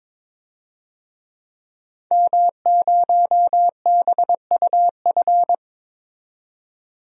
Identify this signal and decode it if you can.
Morse code, heard as a keyed tone, about 22 wpm, 700 Hz. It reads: M0BUF
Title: DXCC-100 Call Signs - 5 Characters - 22wpm